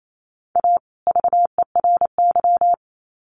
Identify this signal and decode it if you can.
Morse code, heard as a keyed tone, about 28 wpm, 700 Hz. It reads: A VERY
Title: Sets of 2 Words 28wpm